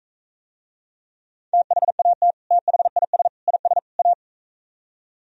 Morse code, heard as a keyed tone, about 45 wpm, 700 Hz. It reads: THAT THIS IS A